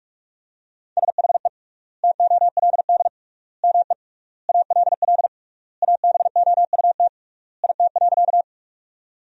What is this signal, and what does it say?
Morse code, heard as a keyed tone, about 41 wpm, 700 Hz. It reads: SHE TOLD ME ALL ABOUT IT.
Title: 3x Word-Spacing - Sentences from Top 400 Words - 45wpm